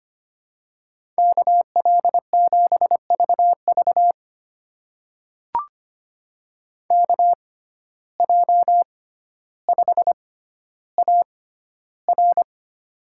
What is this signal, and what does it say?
Morse code, heard as a keyed tone, about 25 wpm, spaced as for 13 wpm, 700 Hz. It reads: KL7VV KJ5AR
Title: USA Call Signs - 5 Characters - 6x Character Spacing - 25wpm